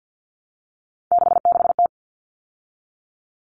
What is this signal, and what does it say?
Morse code, heard as a keyed tone, about 50 wpm, 700 Hz. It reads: 66T
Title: Common Cut Numbers - 3-Digit 50wpm